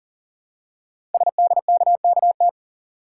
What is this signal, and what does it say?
Morse code, heard as a keyed tone, about 41 wpm, 700 Hz. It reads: SDKKT
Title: Code Groups - 5 Letters - 40wpm